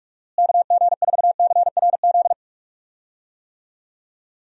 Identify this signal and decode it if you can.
Morse code, heard as a keyed tone, about 46 wpm, 700 Hz. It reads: KG4CRZ
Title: USA Call Signs - 6 Characters - Repeated 2x - 45wpm